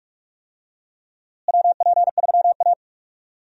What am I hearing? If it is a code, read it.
Morse code, heard as a keyed tone, about 46 wpm, 700 Hz. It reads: WP3A
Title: USA Call Signs - 4 Characters - Repeated 2x - 45wpm